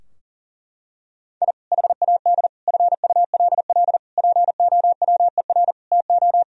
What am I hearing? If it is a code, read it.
Morse code, heard as a keyed tone, about 37 wpm, 700 Hz. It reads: I HAD FULL POWER TO
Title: Sentences from Top 1000 Words - Review Part 3 - 40wpm